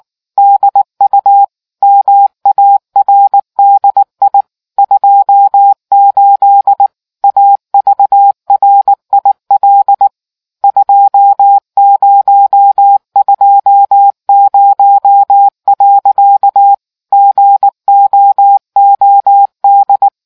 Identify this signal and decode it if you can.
Morse code, heard as a keyed tone, about 19 wpm, 800 Hz. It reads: DU MARDI 28 AVRIL 2020. GOOD